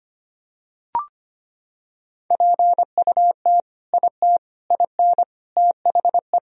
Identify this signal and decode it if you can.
Morse code, heard as a keyed tone, about 25 wpm, 700 Hz. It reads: PUT IT IN THE
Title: Sentences from Top 1000 Words - Review Part 2 - 25wpm